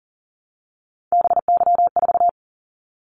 Morse code, heard as a keyed tone, about 40 wpm, 700 Hz. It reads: BY4
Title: Callsign Prefixes for Top 100 DXCC - 40wpm